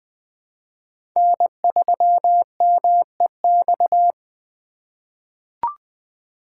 Morse code, heard as a keyed tone, about 20 wpm, 700 Hz. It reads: N3MEX
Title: USA Call Signs - 5 Characters - 3x Character Spacing - 20wpm